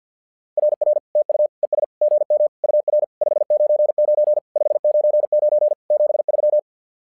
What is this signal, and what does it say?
Morse code, heard as a keyed tone, about 48 wpm, 600 Hz. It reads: RR TU ES GM UR 599 599 73
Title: SOTA QSO Practice 50wpm